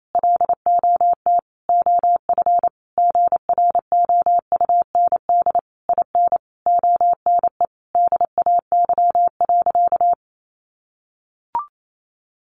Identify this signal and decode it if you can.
Morse code, heard as a keyed tone, about 28 wpm, 700 Hz. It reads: LOT OF GROUND IN ONE DAY.
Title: Sentences from Top 500 Words 28wpm